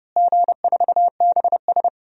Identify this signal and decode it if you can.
Morse code, heard as a keyed tone, about 30 wpm, 700 Hz. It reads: G4BS
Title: DXCC-100 Call Signs - 4 Characters - Repeated 2x - 30wpm